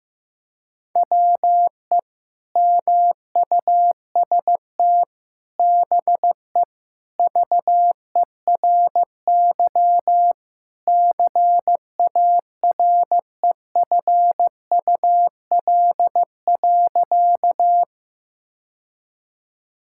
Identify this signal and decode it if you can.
Morse code, heard as a keyed tone, about 15 wpm, 700 Hz. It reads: WE MUST BE VERY CAREFUL.
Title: Sentences from Top 1600 Words 15wpm